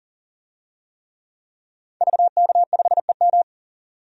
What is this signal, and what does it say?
Morse code, heard as a keyed tone, about 40 wpm, 700 Hz. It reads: VK5EM
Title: DXCC-100 Call Signs - 5 Characters - 40wpm